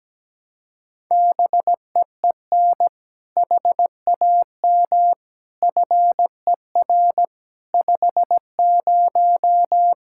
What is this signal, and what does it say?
Morse code, heard as a keyed tone, about 17 wpm, 700 Hz. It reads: BEEN HAM FER 50
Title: K1USN SST QSO Practice 17wpm